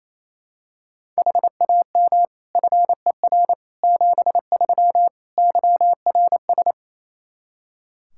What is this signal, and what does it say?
Morse code, heard as a keyed tone, about 28 wpm, 700 Hz. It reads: HAM FER 73 YRS
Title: Practice QSO - Exchange 3 - More Info - 28wpm